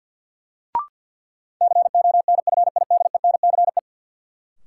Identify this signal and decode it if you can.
Morse code, heard as a keyed tone, about 53 wpm, 700 Hz. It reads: CONFIDENCE
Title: Sentences from Top 1500 Words - New Word Drill - 50wpm